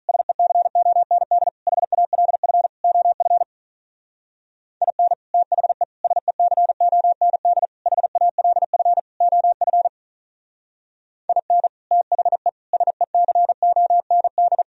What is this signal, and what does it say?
Morse code, heard as a keyed tone, about 44 wpm, 700 Hz. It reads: SECOND HALF OF IN THE SECOND HALF OF IN THE SECOND